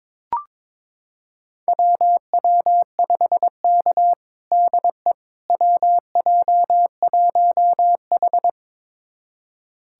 Morse code, heard as a keyed tone, about 22 wpm, 700 Hz. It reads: WW5K DE WJ1H
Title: USA Callsign DE Callsign - 4 Characters - 22wpm